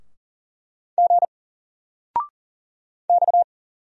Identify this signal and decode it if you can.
Morse code, heard as a keyed tone, about 40 wpm, 700 Hz. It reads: G X
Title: Single Letter-Number 40wpm